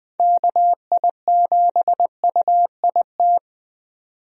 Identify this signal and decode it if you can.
Morse code, heard as a keyed tone, about 20 wpm, 700 Hz. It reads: KI7UIT
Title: USA Call Signs - 6 Characters - 20wpm